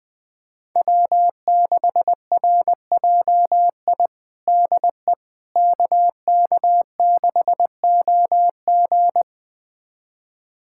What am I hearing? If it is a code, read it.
Morse code, heard as a keyed tone, about 20 wpm, 700 Hz. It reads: W6RJI DE KK6OG
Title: USA Callsign DE Callsign - 5 Characters - 20wpm